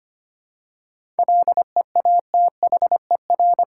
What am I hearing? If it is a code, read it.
Morse code, heard as a keyed tone, about 25 wpm, 700 Hz. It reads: LEATHER
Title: Seven Letter Words - 25wpm